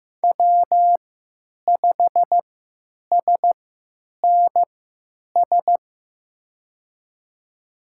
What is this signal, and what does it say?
Morse code, heard as a keyed tone, about 15 wpm, spaced as for 9 wpm, 700 Hz. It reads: W5SNS
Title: USA Call Signs - 5 Characters - 3x Character Spacing - 15wpm